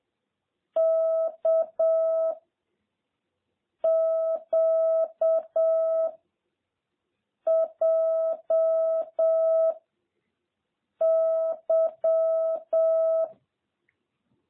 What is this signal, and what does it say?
Morse code, heard as a keyed tone, about 7 wpm, 650 Hz. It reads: K Q J Y
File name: jh8cbh / 2021cw